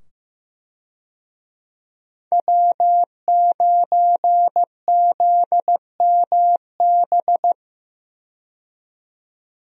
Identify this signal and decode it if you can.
Morse code, heard as a keyed tone, about 15 wpm, 700 Hz. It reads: W9ZMB
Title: USA Call Signs - 5 Characters - 15wpm